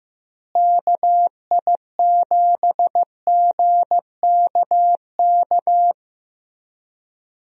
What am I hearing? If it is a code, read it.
Morse code, heard as a keyed tone, about 15 wpm, 700 Hz. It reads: KI7GKK
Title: USA Call Signs - 6 Characters - Repeated 3x - 15wpm